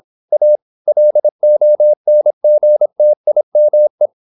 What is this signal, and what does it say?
Morse code, heard as a keyed tone, about 26 wpm, 600 Hz. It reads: A LONGTIME
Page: Never 26wpm | CW med Gnister